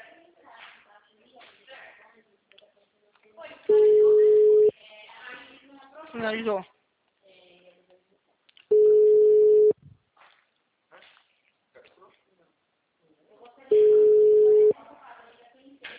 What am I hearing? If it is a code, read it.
Ringback tone (Europe, 425 Hz), with speech.
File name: out- -2003- - - .51346